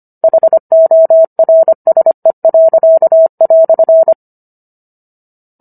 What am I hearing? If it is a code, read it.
Morse code, heard as a keyed tone, about 25 wpm, 650 Hz. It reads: HORSE."